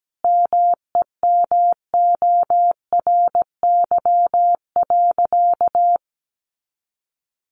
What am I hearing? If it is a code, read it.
Morse code, heard as a keyed tone, about 17 wpm, 700 Hz. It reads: MEMORY.
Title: Sentences from Top 1300 Words 17wpm